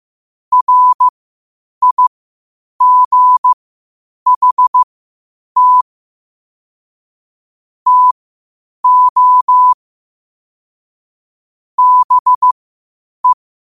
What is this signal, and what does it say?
Morse code, heard as a keyed tone, about 15 wpm, spaced as for 8 wpm, 1000 Hz. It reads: RIGHT TO BE
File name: qotdfiles / 8wpm